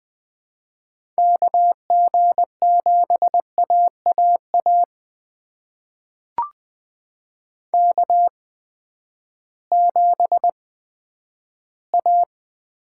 Morse code, heard as a keyed tone, about 20 wpm, spaced as for 11 wpm, 700 Hz. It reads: KG7AAA K7A
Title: USA Call Signs - 8x Character Spacing - 20wpm